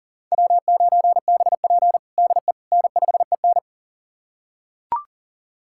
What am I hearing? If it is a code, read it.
Morse code, heard as a keyed tone, about 40 wpm, 700 Hz. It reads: W9BP DE N5EN
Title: USA Callsign DE Callsign - 4 Characters - 40wpm